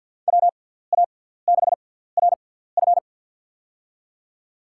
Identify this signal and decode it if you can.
Morse code, heard as a keyed tone, about 48 wpm, spaced as for 17 wpm, 700 Hz. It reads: WA6RF
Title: USA Call Signs - 5 Characters - 6x Character Spacing - 50wpm